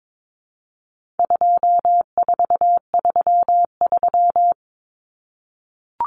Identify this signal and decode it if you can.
Morse code, heard as a keyed tone, about 22 wpm, 700 Hz. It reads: 2433
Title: Common Cut Numbers - 4-Digit 22wpm